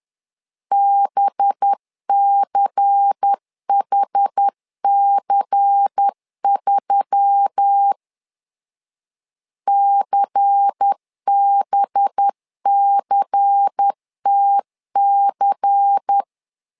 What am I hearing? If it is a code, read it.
Morse code, heard as a keyed tone, about 10 wpm, 800 Hz. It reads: BCHC3 CBCTC
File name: LEZIONI / 5 LBVC / 5 LBVC 10wpm 2349